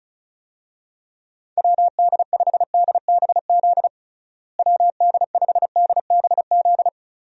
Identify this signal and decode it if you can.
Morse code, heard as a keyed tone, about 34 wpm, 700 Hz. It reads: WD5DBZ WD5DBZ
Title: USA Call Signs - 6 Characters - Repeated 2x - 35wpm